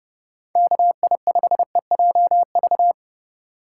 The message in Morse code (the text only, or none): KI5EJV